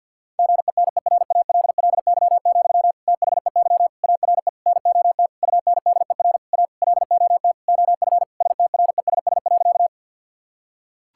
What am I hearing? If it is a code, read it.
Morse code, heard as a keyed tone, about 50 wpm, 700 Hz. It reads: GENERALLY, THEY ARE NOT UNDER A LOT OF STRESS.
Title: Sentences from Top 1000 Words - Review Part 6 - 50wpm